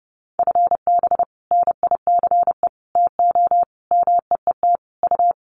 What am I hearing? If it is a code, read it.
Morse code, heard as a keyed tone, about 30 wpm, 700 Hz. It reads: FB NICE TO MEET U
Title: Common QSO Phrases 30wpm